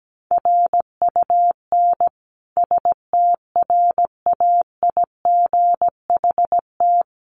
Morse code, heard as a keyed tone, about 17 wpm, 700 Hz. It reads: RUN STRAIGHT
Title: Sentences from Top 1000 Words 17wpm